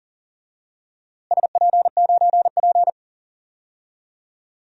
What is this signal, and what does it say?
Morse code, heard as a keyed tone, about 40 wpm, 700 Hz. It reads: SP9P
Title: DXCC-100 Call Signs - 4 Characters - Repeated 2x - 40wpm